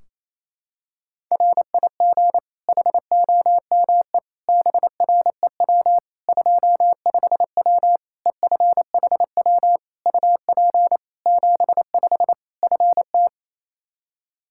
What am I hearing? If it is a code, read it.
Morse code, heard as a keyed tone, about 28 wpm, 700 Hz. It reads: RIG HOME BREW 25W EFHW UP 75 FT
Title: K1USN SST QSO Practice 28wpm